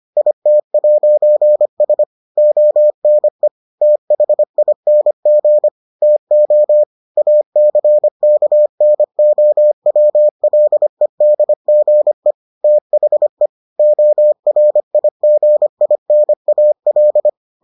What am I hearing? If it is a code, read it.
Morse code, heard as a keyed tone, about 25 wpm, 600 Hz. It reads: IT'S ONE THING TO ACKNOWLEDGE THE ORIGINAL